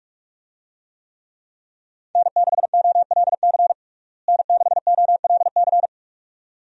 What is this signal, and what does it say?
Morse code, heard as a keyed tone, about 41 wpm, 700 Hz. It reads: N6OLC N6OLC
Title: USA Call Signs - 5 Characters - Repeated 2x - 45wpm